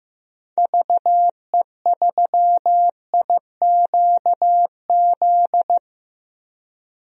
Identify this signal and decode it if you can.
Morse code, heard as a keyed tone, about 15 wpm, 700 Hz. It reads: VE3IQZ
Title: DXCC-100 Call Signs - 6 Characters - Repeated 3x - 15wpm